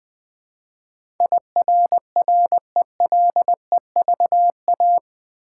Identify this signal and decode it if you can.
Morse code, heard as a keyed tone, about 20 wpm, 700 Hz. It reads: IRRELEVA
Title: Build A Word - 4 Letters At A Time - 10-Letter Words - 20wpm